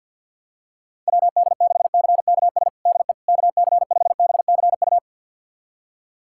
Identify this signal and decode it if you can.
Morse code, heard as a keyed tone, about 51 wpm, 700 Hz. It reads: WD6CKS DE KC5BCU
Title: USA Callsign DE Callsign - 6 Characters - 50wpm